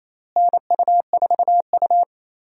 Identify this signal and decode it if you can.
Morse code, heard as a keyed tone, about 28 wpm, 700 Hz. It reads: NU4U
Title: USA Call Signs - 4 Characters - Repeated 3x - 28wpm